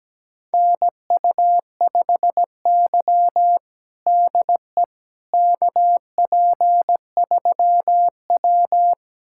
Morse code, heard as a keyed tone, about 17 wpm, 700 Hz. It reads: NU5Y DE KP3W
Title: USA Callsign DE Callsign - 4 Characters - 17wpm